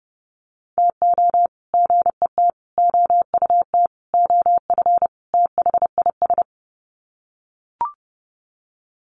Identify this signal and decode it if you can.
Morse code, heard as a keyed tone, about 30 wpm, 700 Hz. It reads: TO GET OUT OF THIS
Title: Sets of 5 Words Encore2 30wpm